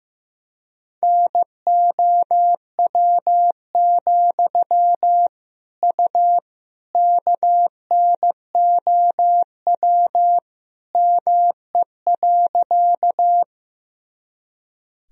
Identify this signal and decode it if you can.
Morse code, heard as a keyed tone, about 15 wpm, 700 Hz. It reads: NOW, U KNOW ME.
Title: Sentences from Top 200 Words and Abbreviations 15wpm